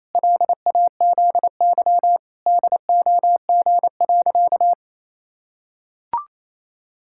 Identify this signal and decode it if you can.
Morse code, heard as a keyed tone, about 28 wpm, 700 Hz. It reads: LAZY DOG.
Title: Pangram Challenge v2 - 28wpm